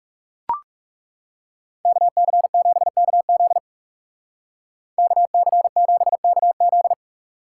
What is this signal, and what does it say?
Morse code, heard as a keyed tone, about 37 wpm, 700 Hz. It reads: KC7KZ KC7KZ
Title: Speed Racing - USA Call Signs 30wpm